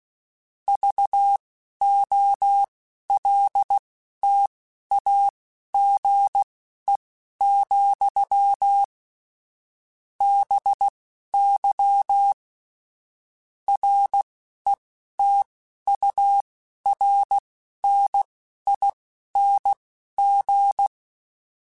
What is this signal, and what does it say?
Morse code, heard as a keyed tone, about 16 wpm, spaced as for 11 wpm, 800 Hz. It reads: VOLTAGE, BY RETURNING
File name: Technical Subjects / Technical Subjects-mp3s / XC @10